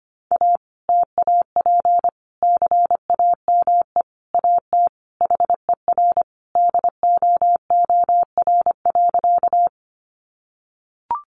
Morse code, heard as a keyed tone, about 25 wpm, 700 Hz. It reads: A TAP CAME AT HER DOOR.
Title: Sentences from Top 1400 Words 25wpm